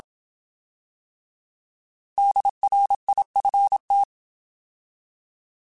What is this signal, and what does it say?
Morse code, heard as a keyed tone, about 27 wpm, 800 Hz. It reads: DRIFT
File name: Alphabetical / Alphabetical mp3s / Ds @25